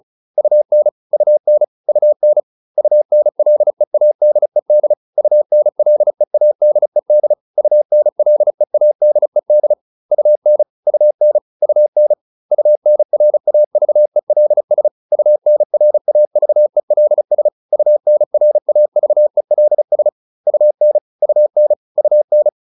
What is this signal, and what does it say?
Morse code, heard as a keyed tone, about 34 wpm, 600 Hz. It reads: UN UN UN UNLEADED UNLEADED UNLEADED UN UN UN UNRAVELS UNRAVELS UNRAVELS UN UN UN
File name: UN8 35